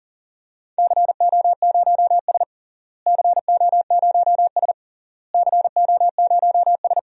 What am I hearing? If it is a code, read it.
Morse code, heard as a keyed tone, about 40 wpm, 700 Hz. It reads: CO0S CO0S CO0S
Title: DXCC-100 Call Signs - 4 Characters - Repeated 3x - 40wpm